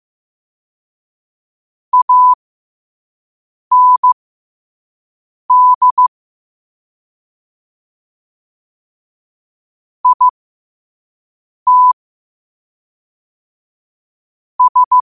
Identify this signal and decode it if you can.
Morse code, heard as a keyed tone, about 15 wpm, spaced as for 5 wpm, 1000 Hz. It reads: AND IT S
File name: 2026-03-16-5wpm